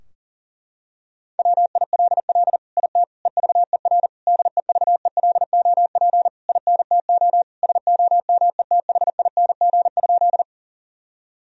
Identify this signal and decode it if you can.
Morse code, heard as a keyed tone, about 40 wpm, 700 Hz. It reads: WILL IT EVER DEVELOP INTO SOMETHING?
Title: Sentences from Top 1000 Words - Review Part 2 - 40wpm